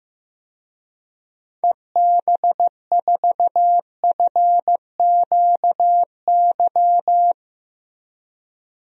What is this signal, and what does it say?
Morse code, heard as a keyed tone, about 15 wpm, 700 Hz. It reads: EB4FQY
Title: DXCC-100 Call Signs - 6 Characters - Repeated 3x - 15wpm